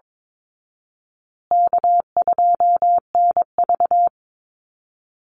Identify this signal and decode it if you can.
Morse code, heard as a keyed tone, about 22 wpm, 700 Hz. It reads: K2NV
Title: Call Signs Repeated 2x - 22wpm